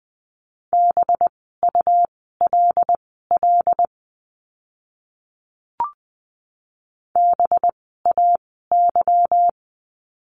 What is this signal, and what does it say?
Morse code, heard as a keyed tone, about 20 wpm, spaced as for 15 wpm, 700 Hz. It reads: BULL BAY
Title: 185 Lesson 30f - Letter b - Words 20wpm